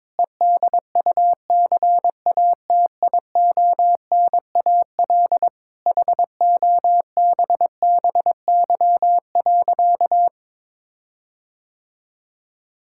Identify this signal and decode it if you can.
Morse code, heard as a keyed tone, about 22 wpm, 700 Hz. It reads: EDUCATIONAL HOBBY.
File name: Difficult Sentences with Hints - 22wpm